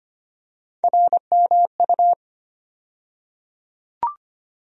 Morse code, heard as a keyed tone, about 25 wpm, 700 Hz. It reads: RMU
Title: USA Call Sign Postfixes - 3 Letters 25wpm